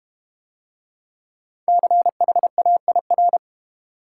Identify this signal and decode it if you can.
Morse code, heard as a keyed tone, about 30 wpm, 700 Hz. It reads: CHAIR
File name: Easy Words - Speed-Racing with Hints - 25wpm